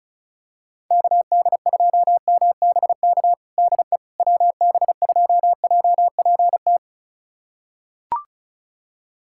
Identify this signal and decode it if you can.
Morse code, heard as a keyed tone, about 36 wpm, 700 Hz. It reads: KD2MBK DE WB2JPT
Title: USA Callsign DE Callsign 35wpm